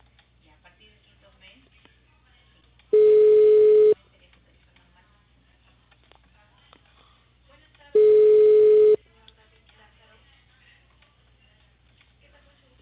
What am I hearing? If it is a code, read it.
Ringback tone (Europe, 425 Hz), with speech.